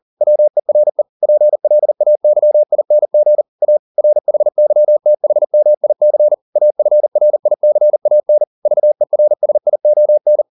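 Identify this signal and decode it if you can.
Morse code, heard as a keyed tone, about 40 wpm, 600 Hz. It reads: WERE PLAYING A RHYTHMIC AFRICAN VERSION
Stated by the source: Morse code at 40 wpm, practice text, CW med Gnister.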